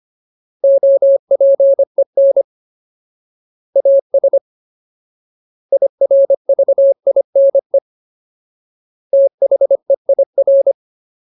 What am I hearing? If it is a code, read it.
Morse code, heard as a keyed tone, about 25 wpm, 550 Hz. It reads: OPEN AS IRVINE THEIR
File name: Ultimate Word List / 25WPM Ultimate